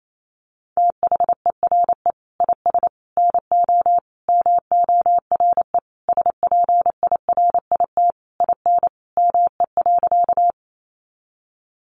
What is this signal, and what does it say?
Morse code, heard as a keyed tone, about 28 wpm, 700 Hz. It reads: THERE IS NO MORE SPIRIT IN ME.
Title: Sentences from Top 1400 Words 28wpm